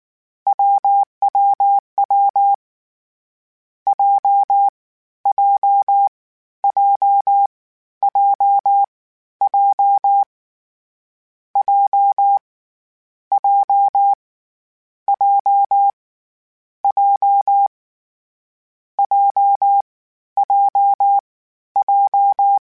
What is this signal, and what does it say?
Morse code, heard as a keyed tone, about 19 wpm, 800 Hz. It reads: WWW J J J J J J J J J J J J
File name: Difficult Combos / Difficult Combos-mp3s / AWJ1 @18